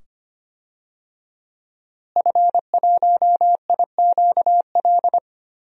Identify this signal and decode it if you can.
Morse code, heard as a keyed tone, about 25 wpm, 700 Hz. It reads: F1IQL
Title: DXCC-100 Call Signs - 5 Characters - Repeated 2x - 25wpm